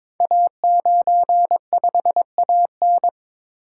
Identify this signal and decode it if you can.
Morse code, heard as a keyed tone, about 22 wpm, 700 Hz. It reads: A95AN
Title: Code Groups - 5 Letter:Numbers - 22wpm